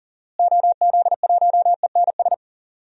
Morse code, heard as a keyed tone, about 39 wpm, 700 Hz. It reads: OZ1ENS
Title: DXCC-100 Call Signs - 6 Characters - Repeated 2x - 40wpm